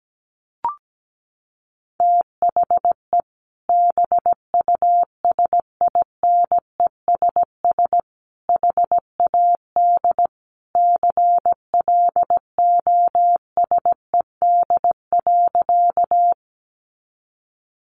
Morse code, heard as a keyed tone, about 17 wpm, 700 Hz. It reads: THE BUSINESS HAD CLOSED.
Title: Sentences from Top 1400 Words 17wpm